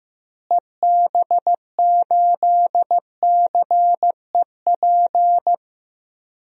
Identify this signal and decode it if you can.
Morse code, heard as a keyed tone, about 15 wpm, 700 Hz. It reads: EB8CEP
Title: DXCC-100 Call Signs - 6 Characters - Repeated 3x - 15wpm